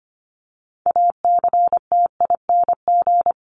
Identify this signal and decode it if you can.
Morse code, heard as a keyed tone, about 25 wpm, 700 Hz. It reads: ACTING
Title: ING Suffix - Top 100 Words with Base Words 25wpm